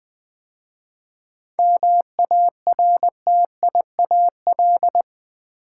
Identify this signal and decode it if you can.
Morse code, heard as a keyed tone, about 20 wpm, 700 Hz. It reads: MARTIAL
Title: 7 Letter Words - First 3 Letters Given - 20wpm